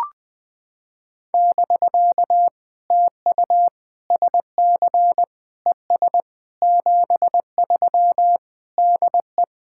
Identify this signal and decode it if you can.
Morse code, heard as a keyed tone, about 20 wpm, 700 Hz. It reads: <BK> TU SC ES 73 DE
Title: POTA QSO Practice - 20wpm